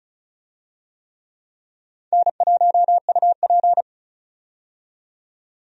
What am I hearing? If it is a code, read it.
Morse code, heard as a keyed tone, about 36 wpm, 700 Hz. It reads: N1UP
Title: USA Call Signs - 4 Characters - 35wpm